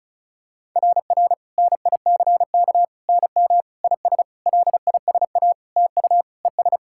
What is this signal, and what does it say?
Morse code, heard as a keyed tone, about 36 wpm, 690 Hz. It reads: RR NICK NM IS LISA TU ES
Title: SOTA QSO Practice 35wpm